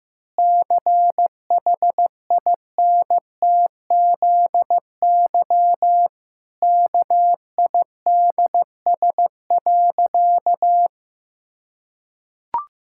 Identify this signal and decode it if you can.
Morse code, heard as a keyed tone, about 15 wpm, 700 Hz. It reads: CHINTZY KIDS.
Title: Pangram Challenge v2 - 15wpm